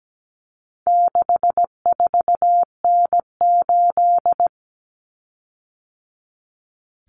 Morse code, heard as a keyed tone, about 17 wpm, 700 Hz. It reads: 64N8
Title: Common Cut Numbers - 4-Digit 17wpm